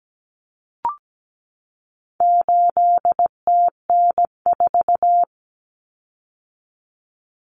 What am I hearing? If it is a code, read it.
Morse code, heard as a keyed tone, about 17 wpm, 700 Hz. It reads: 8TN4
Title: Common Cut Numbers - 4-Digit 17wpm